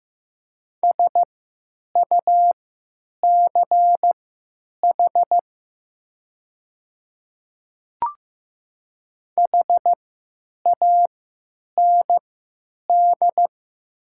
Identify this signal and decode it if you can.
Morse code, heard as a keyed tone, about 15 wpm, spaced as for 8 wpm, 700 Hz. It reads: SUCH HAND
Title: 3x Character Spacing - Top 200 Words - 15wpm